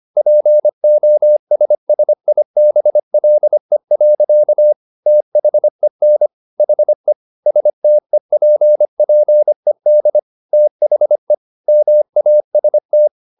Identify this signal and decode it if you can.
Morse code, heard as a keyed tone, about 25 wpm, 600 Hz. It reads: POSSIBLE. THEN HE STEPPED THE MAST